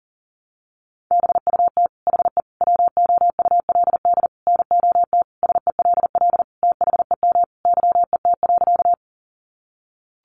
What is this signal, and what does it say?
Morse code, heard as a keyed tone, about 37 wpm, 700 Hz. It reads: BUT HE WOULD NOT SELL THEM YET.
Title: Sentences from Top 1100 Words 40wpm